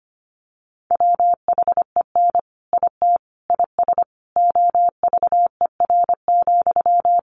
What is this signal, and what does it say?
Morse code, heard as a keyed tone, about 25 wpm, 700 Hz. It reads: WHEN IT IS OVER,
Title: Sentences from Top 1200 Words 25wpm